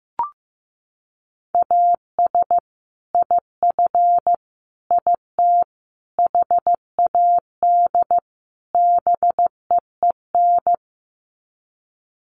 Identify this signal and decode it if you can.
Morse code, heard as a keyed tone, about 14 wpm, 700 Hz. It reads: AS IF IT HAD BEEN
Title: Sets of 5 Words Encore2 15wpm